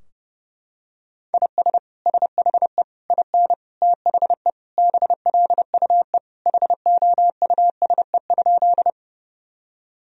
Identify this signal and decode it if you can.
Morse code, heard as a keyed tone, about 30 wpm, 700 Hz. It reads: IS SHE IN THE BLUE HOUSE?
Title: Sentences from Top 1000 Words - Review Part 3 - 30wpm